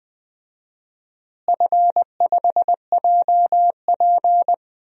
Morse code, heard as a keyed tone, about 20 wpm, 700 Hz. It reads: F5JP
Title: DXCC-100 Call Signs - 4 Characters - Repeated 2x - 20wpm